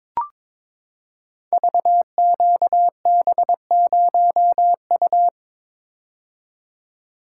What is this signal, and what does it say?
Morse code, heard as a keyed tone, about 22 wpm, 700 Hz. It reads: VQB0U
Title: Code Groups - 5 Letter:Numbers - 22wpm